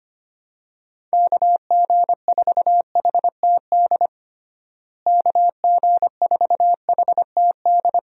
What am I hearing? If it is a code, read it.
Morse code, heard as a keyed tone, about 25 wpm, 700 Hz. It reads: KG4HTD KG4HTD
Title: USA Call Signs - 6 Characters - Repeated 2x - 25wpm